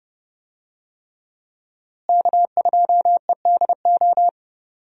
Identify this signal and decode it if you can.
Morse code, heard as a keyed tone, about 32 wpm, 700 Hz. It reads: K2EDO
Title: USA Call Signs - 5 Characters - Repeated 3x - 30wpm